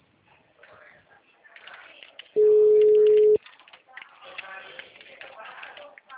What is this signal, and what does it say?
Ringback tone (Europe, 425 Hz), with speech.
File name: out- -2047- - - .62188